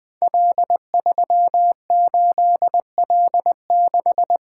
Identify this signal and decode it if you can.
Morse code, heard as a keyed tone, about 20 wpm, 700 Hz. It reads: L38L6
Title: Code Groups - 5 Letter:Numbers - 20wpm